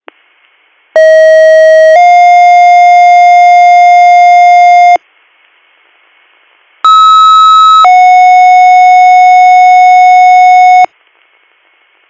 Two-tone page, heard as a one-second tone, then a three-second tone, about 640 then 710 Hz, about 1290 then 730 Hz. It is heard from 1.0 to 10.9 s.